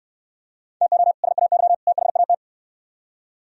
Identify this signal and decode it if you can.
Morse code, heard as a keyed tone, about 67 wpm, 700 Hz. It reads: TO SAY THAT